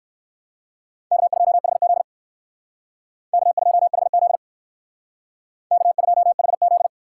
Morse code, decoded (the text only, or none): K2HZ K2HZ K2HZ